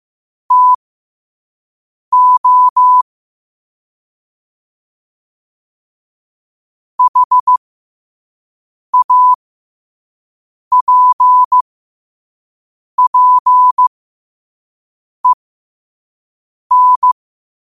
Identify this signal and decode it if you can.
Morse code, heard as a keyed tone, about 15 wpm, spaced as for 5 wpm, 1000 Hz. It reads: TO HAPPEN
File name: qotdfiles / 5wpm